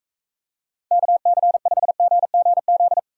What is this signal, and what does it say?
Morse code, heard as a keyed tone, about 42 wpm, 700 Hz. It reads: KC5GGZ